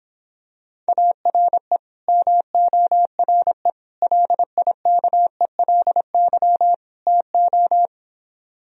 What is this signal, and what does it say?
Morse code, heard as a keyed tone, about 26 wpm, 700 Hz. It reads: ARE MORE LIKELY TO